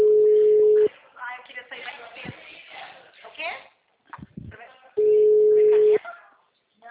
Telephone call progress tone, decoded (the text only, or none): ringback tone